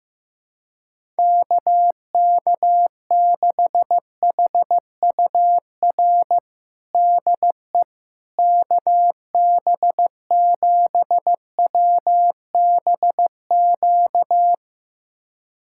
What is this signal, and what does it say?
Morse code, heard as a keyed tone, about 15 wpm, 700 Hz. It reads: KK6HUR DE KB7WBQ
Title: USA Callsign DE Callsign - 6 Characters - 15wpm